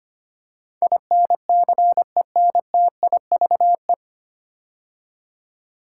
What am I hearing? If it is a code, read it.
Morse code, heard as a keyed tone, about 25 wpm, 700 Hz. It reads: INCENTIVE
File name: practice / 25wpm / 9 Letter Words - First 3 Letters Given - 25wpm